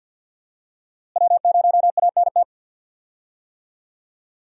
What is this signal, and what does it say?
Morse code, heard as a keyed tone, about 47 wpm, 700 Hz. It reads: W0ANT
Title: USA Call Signs - 5 Characters - Repeated 2x - 50wpm